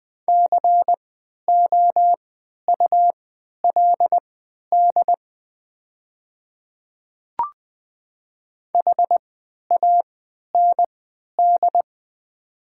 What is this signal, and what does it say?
Morse code, heard as a keyed tone, about 20 wpm, spaced as for 11 wpm, 700 Hz. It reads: COULD HAND
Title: 3x Character Spacing - Top 200 Words - 20wpm